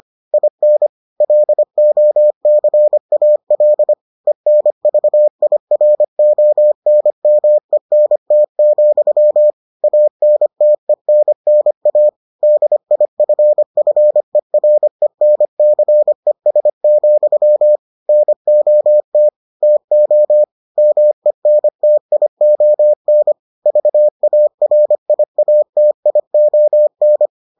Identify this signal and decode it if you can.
Morse code, heard as a keyed tone, about 25 wpm, 600 Hz. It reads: IN LOCAL ENVIRONMENT, ANTENNA DIFFERENCES, NOT TO MENTION VARIATION